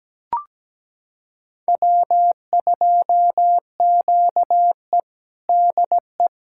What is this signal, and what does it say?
Morse code, heard as a keyed tone, about 17 wpm, 700 Hz. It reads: W2QE DE
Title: USA Callsign DE Callsign - 4 Characters - 17wpm